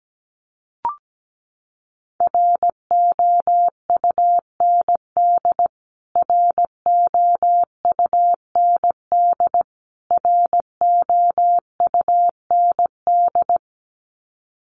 Morse code, heard as a keyed tone, about 17 wpm, 700 Hz. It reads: ROUND ROUND ROUND
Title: Top 500 Words Repeated 17wpm